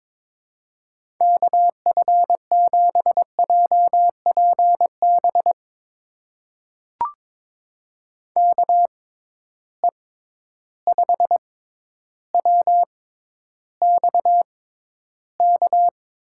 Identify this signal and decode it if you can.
Morse code, heard as a keyed tone, about 23 wpm, spaced as for 11 wpm, 700 Hz. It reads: KF7JPB KE5WXK
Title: USA Call Signs - 6 Characters - 6x Character Spacing - 22wpm